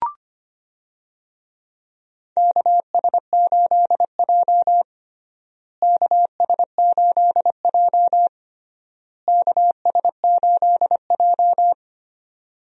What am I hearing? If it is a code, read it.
Morse code, heard as a keyed tone, about 25 wpm, 700 Hz. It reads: KS8J KS8J KS8J
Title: USA Call Signs - 4 Characters - Repeated 3x - 25wpm